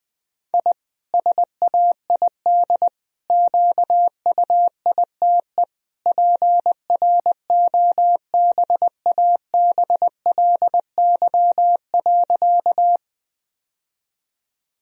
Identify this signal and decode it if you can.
Morse code, heard as a keyed tone, about 20 wpm, 700 Hz. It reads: I SAID QUITE PROBABLY.
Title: Sentences from Top 1000 Words - Review Part 4 - 20wpm